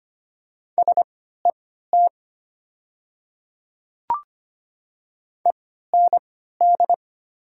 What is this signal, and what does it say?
Morse code, heard as a keyed tone, about 25 wpm, spaced as for 14 wpm, 700 Hz. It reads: SET END
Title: 3x Character Spacing - Top 300 Words - 25wpm